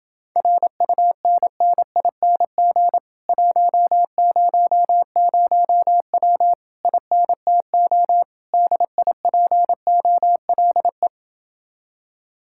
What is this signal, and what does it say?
Morse code, heard as a keyed tone, about 30 wpm, 700 Hz. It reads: RUNNING 100W INTO DIPOLE